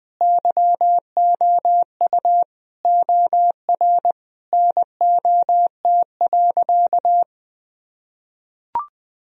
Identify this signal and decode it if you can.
Morse code, heard as a keyed tone, about 20 wpm, 700 Hz. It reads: YOU OR NOT.
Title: Sentences from Top 1000 Words - Review Part 3 - 20wpm